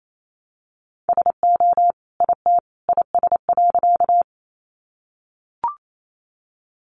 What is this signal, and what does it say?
Morse code, heard as a keyed tone, about 28 wpm, 700 Hz. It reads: SO IT IS.
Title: Sentences from Top 100 Words 28wpm